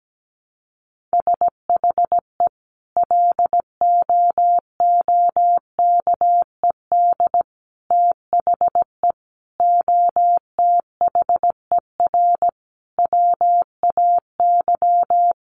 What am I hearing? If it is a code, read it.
Morse code, heard as a keyed tone, about 17 wpm, 700 Hz. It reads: SHE LOOKED THE OTHER WAY
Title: Sentences from Top 800 Words 17wpm